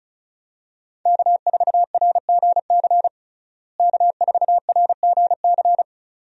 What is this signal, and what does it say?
Morse code, heard as a keyed tone, about 36 wpm, 700 Hz. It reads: K4RGC K4RGC
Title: USA Call Signs - 5 Characters - Repeated 2x - 35wpm